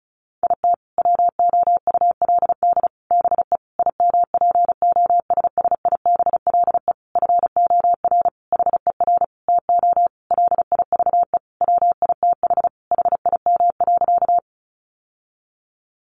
Morse code, heard as a keyed tone, about 34 wpm, 700 Hz. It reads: IT WOULD BE IMPOSSIBLE FOR HER TO LIVE WITH HIM.
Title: Sentences from Top 1100 Words 35wpm